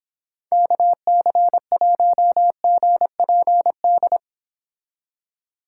Morse code, heard as a keyed tone, about 26 wpm, 700 Hz. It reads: KC1GPD